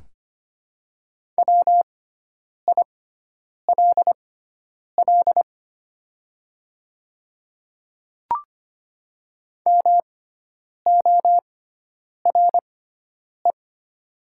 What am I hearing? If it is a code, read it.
Morse code, heard as a keyed tone, about 25 wpm, spaced as for 9 wpm, 700 Hz. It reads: WILL MORE
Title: 6x Character Spacing - Top 100 Words - 25wpm